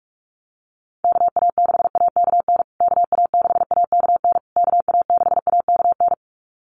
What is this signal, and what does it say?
Morse code, heard as a keyed tone, about 44 wpm, 700 Hz. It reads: KA6AKN KA6AKN KA6AKN
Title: USA Call Signs Repeated 3x 45wpm